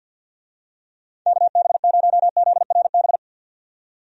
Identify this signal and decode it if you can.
Morse code, heard as a keyed tone, about 48 wpm, 700 Hz. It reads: KB0ZRB
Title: USA Call Signs - 6 Characters - 50wpm